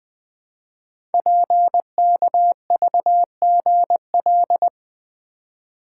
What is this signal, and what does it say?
Morse code, heard as a keyed tone, about 20 wpm, 700 Hz. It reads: PKVGL
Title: Code Groups - 5 Letters - 20wpm